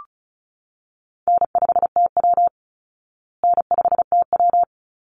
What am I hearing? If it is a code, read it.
Morse code, heard as a keyed tone, about 34 wpm, 700 Hz. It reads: N5TW N5TW
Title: Easy Call Signs Repeated 2x - 35wpm